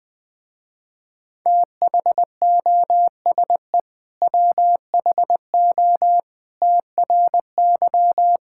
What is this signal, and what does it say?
Morse code, heard as a keyed tone, about 20 wpm, 700 Hz. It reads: THOSE WHO TRY
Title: Sentences with 2-Word Phrases - Spoken Beginning - 20wpm